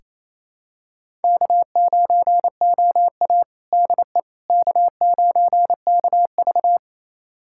Morse code, heard as a keyed tone, about 27 wpm, 700 Hz. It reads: K9OA DE K9KV
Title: USA Callsign DE Callsign - 4 Characters - 28wpm